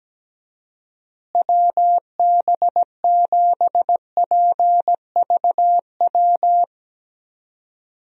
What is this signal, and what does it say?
Morse code, heard as a keyed tone, about 17 wpm, 700 Hz. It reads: WB7PVW
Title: USA Call Signs - 6 Characters - Repeated 2x - 17wpm